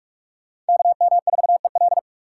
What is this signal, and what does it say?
Morse code, heard as a keyed tone, about 57 wpm, 700 Hz. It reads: KM4EL